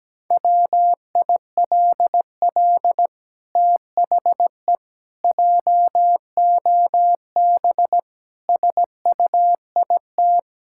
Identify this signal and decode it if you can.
Morse code, heard as a keyed tone, about 17 wpm, 700 Hz. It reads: WILL THE JOB SUIT
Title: Sentences from Top 1600 Words 17wpm